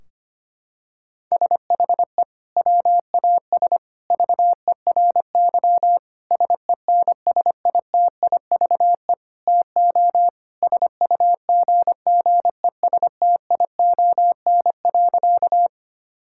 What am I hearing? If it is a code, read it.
Morse code, heard as a keyed tone, about 25 wpm, 700 Hz. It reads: SHE WAS VERY SENSITIVE TO SUGGESTION.
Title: Sentences from Top 1600 Words 25wpm